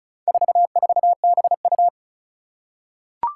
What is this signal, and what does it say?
Morse code, heard as a keyed tone, about 36 wpm, 700 Hz. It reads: 44BU
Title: Four Challenging Letter-Number Combinations 35wpm